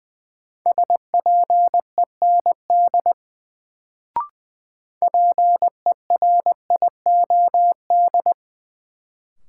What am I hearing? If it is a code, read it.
Morse code, heard as a keyed tone, about 20 wpm, 700 Hz. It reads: SPEND PERIOD
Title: Sets of 5 Words - New Word Drill - 20wpm